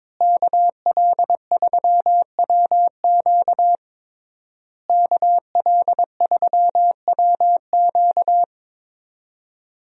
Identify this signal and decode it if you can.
Morse code, heard as a keyed tone, about 22 wpm, 700 Hz. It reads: KL3WQ KL3WQ
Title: USA Call Signs - 5 Characters - Repeated 3x - 22wpm